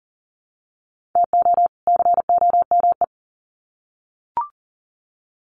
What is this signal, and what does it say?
Morse code, heard as a keyed tone, about 40 wpm, 700 Hz. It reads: TO COME
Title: Sets of 2 Words 40wpm